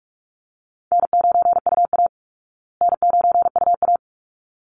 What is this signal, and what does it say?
Morse code, heard as a keyed tone, about 44 wpm, 700 Hz. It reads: N9UA N9UA
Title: Easy Call Signs Repeated 2x - 45wpm